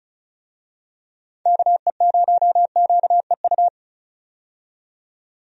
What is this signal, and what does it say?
Morse code, heard as a keyed tone, about 36 wpm, 700 Hz. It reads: KE0QEU
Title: USA Call Signs - 6 Characters - 35wpm